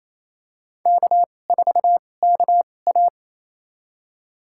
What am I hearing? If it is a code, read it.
Morse code, heard as a keyed tone, about 28 wpm, spaced as for 20 wpm, 700 Hz. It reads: K4KA
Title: USA Call Signs - 2x Character Spacing - 28wpm